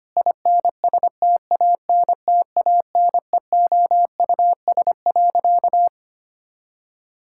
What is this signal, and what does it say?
Morse code, heard as a keyed tone, about 25 wpm, 700 Hz. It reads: INSTANTANEOUS.
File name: practice / 25wpm / Difficult Sentences with Hints - 25wpm